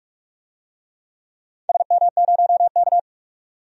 Morse code, heard as a keyed tone, about 46 wpm, 700 Hz. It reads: SM0K
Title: DXCC-100 Call Signs - 4 Characters - 45wpm